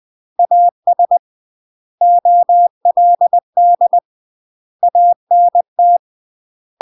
Morse code, heard as a keyed tone, about 20 wpm, 700 Hz. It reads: AS OLD ANT
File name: UK MASTER QSOs TRAINING/20WPM